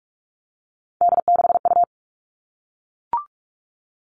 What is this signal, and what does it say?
Morse code, heard as a keyed tone, about 44 wpm, 700 Hz. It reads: D6U
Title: All Cut Numbers - 3-Digit 45wpm